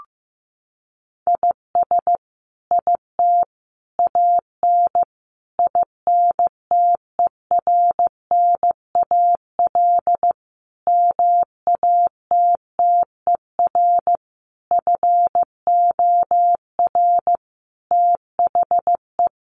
Morse code, heard as a keyed tone, about 15 wpm, 700 Hz. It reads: IS IT AN INTERNAL MATTER FOR THE
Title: Sentences from Top 1400 Words 15wpm